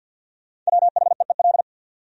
Morse code, heard as a keyed tone, about 60 wpm, 700 Hz. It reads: WHEEL
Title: Speed-Racing Top 500 Words 50wpm